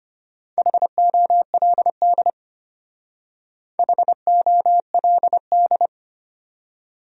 Morse code, heard as a keyed tone, about 33 wpm, 700 Hz. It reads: HOLD HOLD